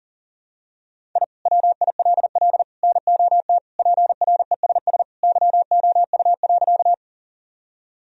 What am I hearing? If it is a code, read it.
Morse code, heard as a keyed tone, about 40 wpm, 700 Hz. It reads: I WILL NOT PRESS YOU.
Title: Sentences from Top 1000 Words - Review Part 1 - 40wpm